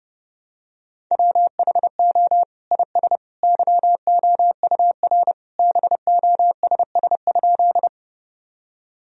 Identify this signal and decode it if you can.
Morse code, heard as a keyed tone, about 30 wpm, 700 Hz. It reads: WHO IS YOUR BOSS?
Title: Sentences from Top 1000 Words - Review Part 4 - 30wpm